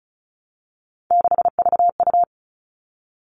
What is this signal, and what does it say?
Morse code, heard as a keyed tone, about 34 wpm, 700 Hz. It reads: 6VU
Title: All Cut Numbers - 3-Digit 35wpm